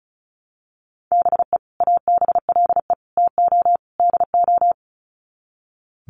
Morse code, heard as a keyed tone, about 34 wpm, 700 Hz. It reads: BE ABLE TO DO
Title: Sets of 4 Words 35wpm